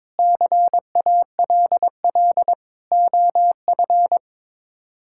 Morse code, heard as a keyed tone, about 22 wpm, 700 Hz. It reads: CALL OF
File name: practice / 22wpm / Sets of 8 Words - 22wpm